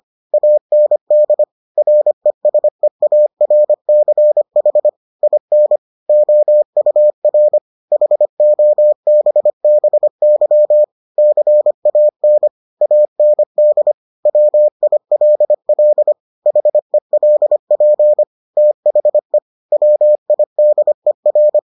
Morse code, heard as a keyed tone, about 25 wpm, 600 Hz. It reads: AND RESEARCH IN OUR HOBBY CAN AND WILL HELP THE WIDER